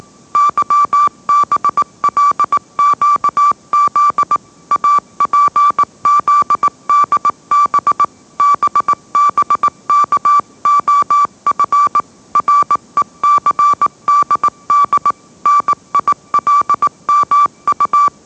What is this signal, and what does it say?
Morse code, heard as a keyed tone, about 18 wpm, 1160 Hz. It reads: YBLQZ APZDB BBKOF RECDD NILMU